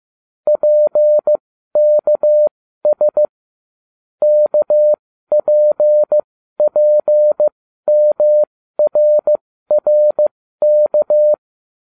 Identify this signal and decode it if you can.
Morse code, heard as a keyed tone, about 15 wpm, spaced as for 12 wpm, 600 Hz. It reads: PKS KPPMRRK